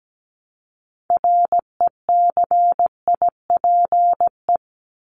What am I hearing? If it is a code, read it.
Morse code, heard as a keyed tone, about 17 wpm, 700 Hz. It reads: RECIPE
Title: Top 1001-1250 Words 17wpm